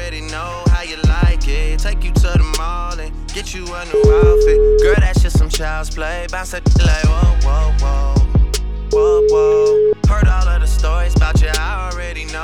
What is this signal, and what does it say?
Ringback tone (Europe, 425 Hz), with noise.